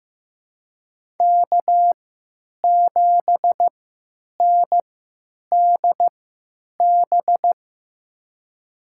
Morse code, heard as a keyed tone, about 16 wpm, spaced as for 9 wpm, 700 Hz. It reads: K7NDB
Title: USA Call Signs - 3x Character Spacing - 15wpm